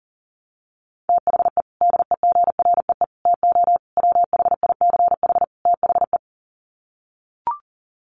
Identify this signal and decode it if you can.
Morse code, heard as a keyed tone, about 40 wpm, 700 Hz. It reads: THE DEGREE TO WHICH THE
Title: Sets of 5 Words Encore2 40wpm